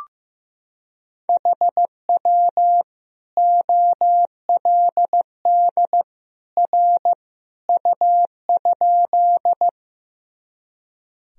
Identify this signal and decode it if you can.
Morse code, heard as a keyed tone, about 16 wpm, 700 Hz. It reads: HW OLD R U?
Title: Sentences from Top 200 Words and Abbreviations 15wpm